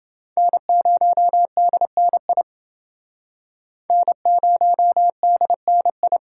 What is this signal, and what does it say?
Morse code, heard as a keyed tone, about 28 wpm, 700 Hz. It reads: N0DNI N0DNI